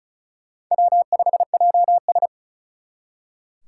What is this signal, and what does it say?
Morse code, heard as a keyed tone, about 44 wpm, 700 Hz. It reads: W5JS